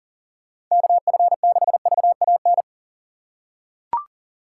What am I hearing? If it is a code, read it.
Morse code, heard as a keyed tone, about 38 wpm, 700 Hz. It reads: KF6VAN
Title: USA Call Signs - 6 Characters - 6x Character Spacing - 40wpm